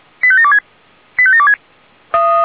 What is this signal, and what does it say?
Five-tone selective call (ZVEI), heard as quick runs of five tones, tones of about 70 ms. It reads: 85726 85726
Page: /464/SDR/ZVEI/